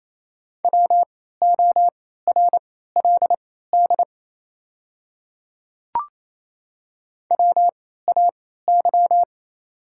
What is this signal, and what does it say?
Morse code, heard as a keyed tone, about 28 wpm, spaced as for 16 wpm, 700 Hz. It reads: WORLD WAY
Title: 3x Character Spacing - Top 300 Words - 28wpm